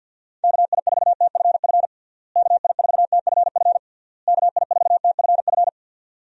Morse code, heard as a keyed tone, about 51 wpm, 700 Hz. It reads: KI4TFF KI4TFF KI4TFF
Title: USA Call Signs - 6 Characters - Repeated 3x - 50wpm